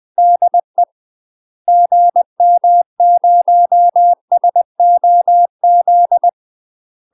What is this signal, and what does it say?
Morse code, heard as a keyed tone, about 20 wpm, 700 Hz. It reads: DE GM0SOZ
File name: UK MASTER QSOs TRAINING / 20WPM / UK MASTER QSOs 0003